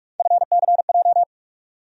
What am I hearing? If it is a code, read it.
Morse code, heard as a keyed tone, about 42 wpm, 700 Hz. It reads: FCJ
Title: USA Call Sign Postfixes - 3 Letters 45wpm